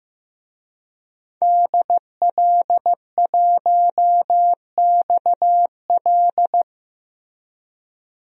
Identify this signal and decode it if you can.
Morse code, heard as a keyed tone, about 15 wpm, 700 Hz. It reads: DL1XL
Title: DXCC-100 Call Signs - 5 Characters - 15wpm